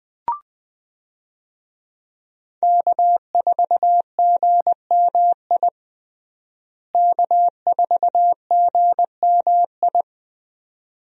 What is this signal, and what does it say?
Morse code, heard as a keyed tone, about 20 wpm, 700 Hz. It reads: K4GMI K4GMI
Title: USA Call Signs - 5 Characters - Repeated 2x - 20wpm